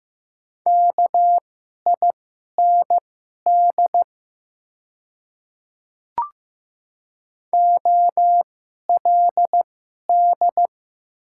Morse code, heard as a keyed tone, about 15 wpm, spaced as for 11 wpm, 690 Hz. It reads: KIND OLD
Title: 2x Character Spacing - Top 200 Words - 15wpm